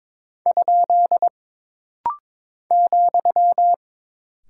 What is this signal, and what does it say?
Morse code, heard as a keyed tone, about 22 wpm, 700 Hz. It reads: ? ,
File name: practice / 22wpm / Single Punctuation-Prosign - Rapid-Fire - 22wpm